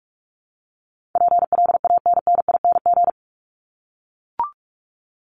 Morse code, heard as a keyed tone, about 44 wpm, 700 Hz. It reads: PLANNING
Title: ING Suffix - Top 100 Words with Base Words 45wpm